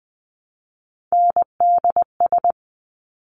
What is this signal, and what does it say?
Morse code, heard as a keyed tone, about 20 wpm, 700 Hz. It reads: NDS
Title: Top Three Letter Combinations in Words 20wpm